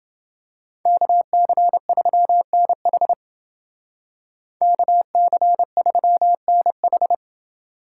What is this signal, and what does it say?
Morse code, heard as a keyed tone, about 29 wpm, 700 Hz. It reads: KC3NH KC3NH